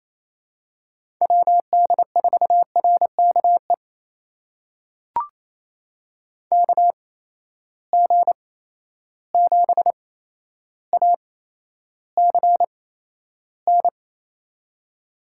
Morse code, heard as a keyed tone, about 27 wpm, spaced as for 12 wpm, 700 Hz. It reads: WD4RKE KG7ACN
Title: USA Call Signs - 6 Characters - 8x Character Spacing - 28wpm